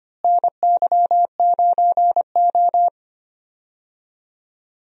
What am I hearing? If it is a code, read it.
Morse code, heard as a keyed tone, about 25 wpm, 700 Hz. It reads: NY9O
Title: USA Call Signs - 4 Characters - Repeated 3x - 25wpm